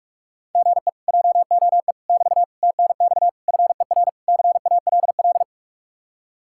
Morse code, heard as a keyed tone, about 50 wpm, 700 Hz. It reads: GE JOE <BT> TNX FER CALL